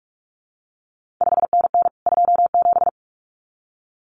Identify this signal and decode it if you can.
Morse code, heard as a keyed tone, about 46 wpm, 700 Hz. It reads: 5NN 27
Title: CQ WW DX Contest Exchange Encore - 45wpm